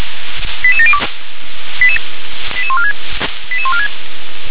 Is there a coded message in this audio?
Yes: five-tone selective call (ZVEI) from 0.6 to 3.9 s, heard as quick runs of five tones, tones of about 70 ms. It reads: 88702 88146 88146